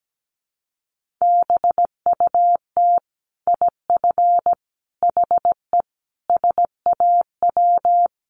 Morse code, heard as a keyed tone, about 17 wpm, 700 Hz. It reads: BUT IF HE SAW
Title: Sentences from Top 1400 Words 17wpm